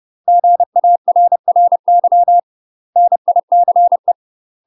Morse code, heard as a keyed tone, about 30 wpm, 700 Hz. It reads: GARRY NICE
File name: UK MASTER QSOs TRAINING / 30WPM / UK MASTER QSOs 0003